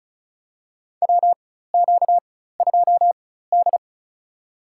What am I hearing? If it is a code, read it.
Morse code, heard as a keyed tone, about 36 wpm, spaced as for 16 wpm, 700 Hz. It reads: WQ2D
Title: USA Call Signs - 4 Characters - 4x Character Spacing - 35wpm